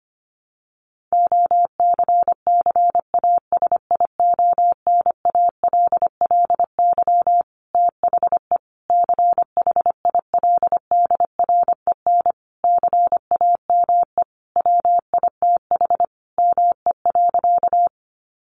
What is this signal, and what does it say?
Morse code, heard as a keyed tone, about 25 wpm, 700 Hz. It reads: OCCASIONALLY THE CHILDREN CAME WITH ME.
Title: Sentences from Top 1300 Words 25wpm